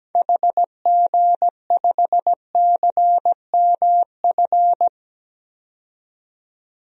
Morse code, heard as a keyed tone, about 17 wpm, 700 Hz. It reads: HG5CMF
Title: DXCC-100 Call Signs - 6 Characters - 17wpm